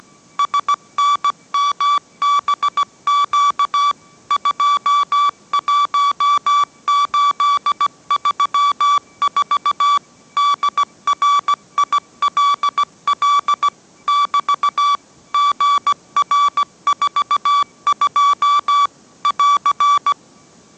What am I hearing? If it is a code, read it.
Morse code, heard as a keyed tone, about 17 wpm, 1150 Hz. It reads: SNMBQ 21834 DRILL <BT> GR42 <AR>